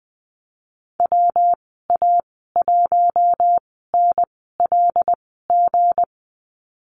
Morse code, heard as a keyed tone, about 20 wpm, spaced as for 15 wpm, 700 Hz. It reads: WA1NLG
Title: 167 Lesson 27g - Number 7 - Call Signs 20wpm